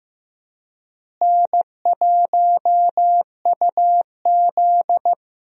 Morse code, heard as a keyed tone, about 15 wpm, 700 Hz. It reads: N1UZ
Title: USA Call Signs - 4 Characters - Repeated 2x - 15wpm